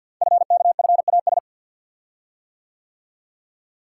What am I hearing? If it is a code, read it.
Morse code, heard as a keyed tone, about 51 wpm, 700 Hz. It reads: FKFAS
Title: Code Groups - 5 Letter:Numbers - 50wpm